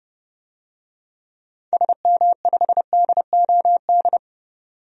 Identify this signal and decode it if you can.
Morse code, heard as a keyed tone, about 30 wpm, 700 Hz. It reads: SM5DOD
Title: DXCC-100 Call Signs - 6 Characters - 30wpm